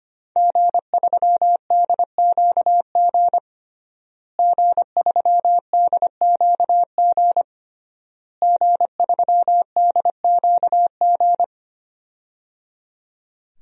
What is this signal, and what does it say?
Morse code, heard as a keyed tone, about 25 wpm, 700 Hz. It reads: G3DQG G3DQG G3DQG
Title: DXCC-100 Call Signs - 5 Characters - Repeated 3x - 25wpm